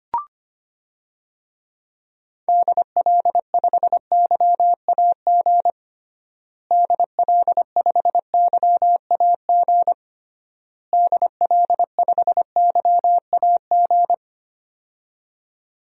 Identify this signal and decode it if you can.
Morse code, heard as a keyed tone, about 25 wpm, 700 Hz. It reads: DL5YAG DL5YAG DL5YAG
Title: DXCC-100 Call Signs - 6 Characters - Repeated 3x - 25wpm